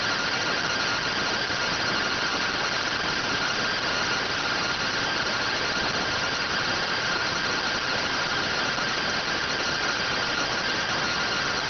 Two-tone teleprinter signal, about 170 Hz shift.